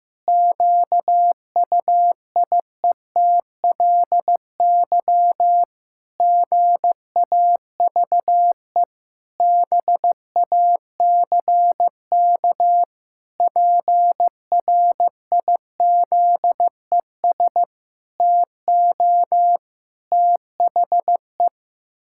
Morse code, heard as a keyed tone, about 15 wpm, 700 Hz. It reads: QUIETLY GAVE BACK PRIZES TO THE
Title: Pangram Challenge v2 - 15wpm